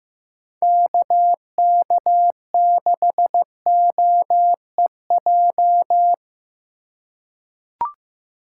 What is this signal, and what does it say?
Morse code, heard as a keyed tone, about 15 wpm, 700 Hz. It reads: KK6OEJ
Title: USA Call Signs - 6 Characters - 3x Character Spacing - 15wpm